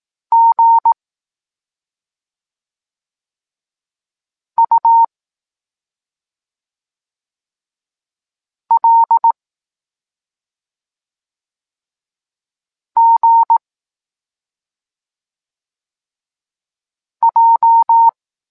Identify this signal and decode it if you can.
Morse code, heard as a keyed tone, about 18 wpm, 920 Hz. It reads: G U L G J
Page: Tekst 23.3 – Ciąg dalszy doskonalenia odbioru poznanych liter